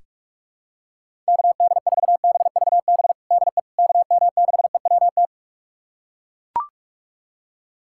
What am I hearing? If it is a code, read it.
Morse code, heard as a keyed tone, about 41 wpm, 700 Hz. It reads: KD4BVB DE KM6EWT
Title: USA Callsign DE Callsign - 6 Characters - 45wpm